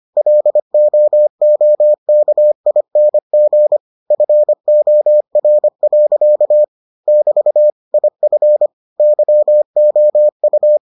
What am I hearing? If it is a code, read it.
Morse code, heard as a keyed tone, about 25 wpm, 600 Hz. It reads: LOOKING FOR. <BT> IF YOU